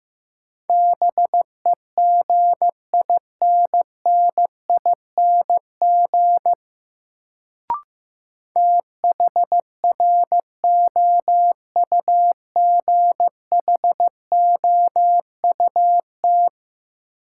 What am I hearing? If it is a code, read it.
Morse code, heard as a keyed tone, about 15 wpm, 700 Hz. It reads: BEGINNING THROUGHOUT
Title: Sentences from Top 800 Words - New Word Drill - 15wpm